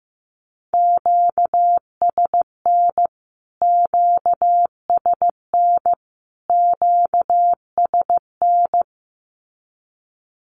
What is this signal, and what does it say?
Morse code, heard as a keyed tone, about 15 wpm, 700 Hz. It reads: QSN QSN QSN
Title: Q-Codes Repeated 3x 15wpm